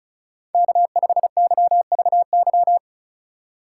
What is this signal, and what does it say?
Morse code, heard as a keyed tone, about 34 wpm, 700 Hz. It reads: K5YVY
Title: K1USN SST QSO Practice - 35wpm